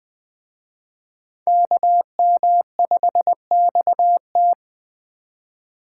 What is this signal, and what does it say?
Morse code, heard as a keyed tone, about 20 wpm, 700 Hz. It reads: KM5XT
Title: Speed Racing - USA 2x2 Call Signs 17wpm